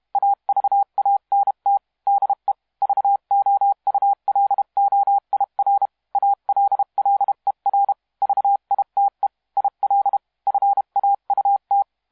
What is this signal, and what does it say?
Morse code, heard as a keyed tone, about 32 wpm, 800 Hz. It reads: AVANT DE VOULOIR ALLER VITE IL FAUT
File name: musiqueradio / cw hsc demo f6ddr